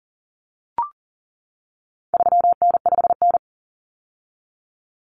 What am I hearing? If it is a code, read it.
Morse code, heard as a keyed tone, about 40 wpm, 700 Hz. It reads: 3N5N
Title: Common Cut Numbers - 4-Digit 40wpm